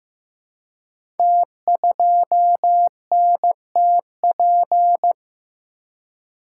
Morse code, heard as a keyed tone, about 15 wpm, 700 Hz. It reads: T2NTP
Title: Code Groups - 5 Letter:Numbers - 15wpm